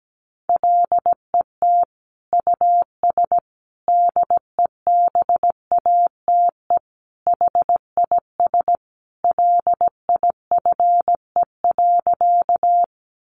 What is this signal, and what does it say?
Morse code, heard as a keyed tone, about 17 wpm, 700 Hz. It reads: LET US DEBATE HIS LIFE.
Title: Sentences from Top 1100 Words 17wpm